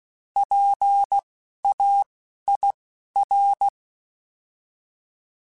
Morse code, heard as a keyed tone, about 16 wpm, 800 Hz. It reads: P A I R
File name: Alphabetical / Alphabetical mp3s / Ps @10